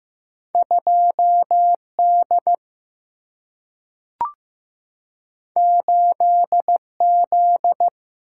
Morse code, heard as a keyed tone, about 15 wpm, 700 Hz. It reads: 2D 8Z
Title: Single-Letter Single-Number 15wpm